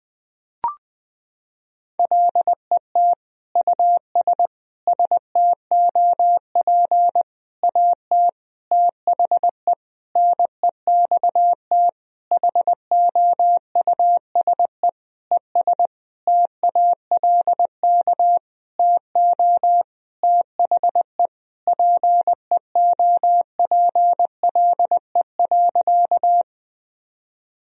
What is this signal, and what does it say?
Morse code, heard as a keyed tone, about 20 wpm, 700 Hz. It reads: LET US STOP AT THE NEXT HOUSE ES TALK TO THE PEOPLE.
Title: Sentences from Top 400 Words and Abbreviations 20wpm